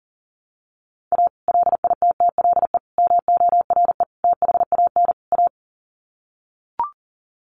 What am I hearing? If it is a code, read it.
Morse code, heard as a keyed tone, about 37 wpm, 700 Hz. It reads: A LITTLE MORE THAN A
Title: Sets of 5 Words Encore2 40wpm